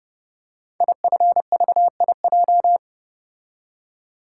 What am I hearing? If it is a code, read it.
Morse code, heard as a keyed tone, about 30 wpm, 700 Hz. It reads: IFVIJ
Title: Code Groups - 5 Letters - 30wpm